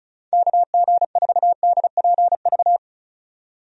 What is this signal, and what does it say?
Morse code, heard as a keyed tone, about 36 wpm, 700 Hz. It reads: KG4DPV
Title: USA Call Signs - 6 Characters - 35wpm